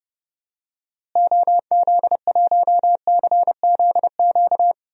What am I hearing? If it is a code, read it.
Morse code, heard as a keyed tone, about 30 wpm, 700 Hz. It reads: OZ1CZQ
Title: DXCC-100 Call Signs - 6 Characters - Repeated 2x - 30wpm